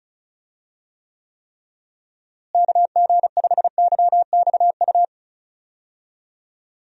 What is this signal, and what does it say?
Morse code, heard as a keyed tone, about 36 wpm, 700 Hz. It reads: KG5YXU
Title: USA Call Signs - 6 Characters - Repeated 3x - 35wpm